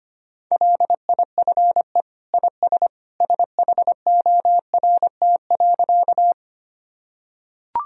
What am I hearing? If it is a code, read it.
Morse code, heard as a keyed tone, about 25 wpm, 700 Hz. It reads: LIFE IS SHORT.
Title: Sentences from Top 1000 Words - Review Part 2 - 25wpm